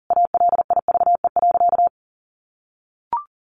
Morse code, heard as a keyed tone, about 40 wpm, 700 Hz. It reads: ALIVE.
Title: A Sound Of Thunder 40wpm